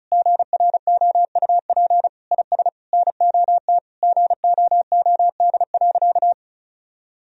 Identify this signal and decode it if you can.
Morse code, heard as a keyed tone, about 34 wpm, 700 Hz. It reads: GROUP IS NOT GOOD.
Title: Sentences from Top 1000 Words - Review Part 1 - 35wpm